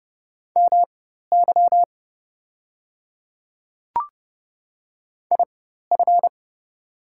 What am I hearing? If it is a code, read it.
Morse code, heard as a keyed tone, about 30 wpm, spaced as for 14 wpm, 700 Hz. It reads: MY IF
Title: 4x Character Spacing - Top 100 Words - 30wpm